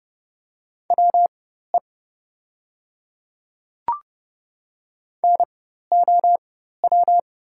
Morse code, heard as a keyed tone, about 30 wpm, spaced as for 14 wpm, 700 Hz. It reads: WE NOW
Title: 4x Character Spacing - Top 100 Words - 30wpm